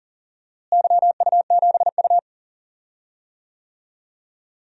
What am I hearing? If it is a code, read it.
Morse code, heard as a keyed tone, about 38 wpm, 700 Hz. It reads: YU7U
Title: DXCC-100 Call Signs - 4 Characters - Repeated 2x - 40wpm